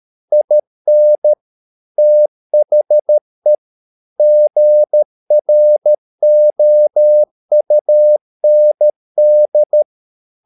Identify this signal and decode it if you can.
Morse code, heard as a keyed tone, about 13 wpm, 600 Hz. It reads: IN THE GROUND